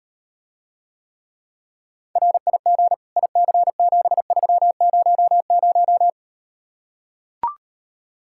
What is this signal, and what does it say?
Morse code, heard as a keyed tone, about 38 wpm, 700 Hz. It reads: RIG IC7300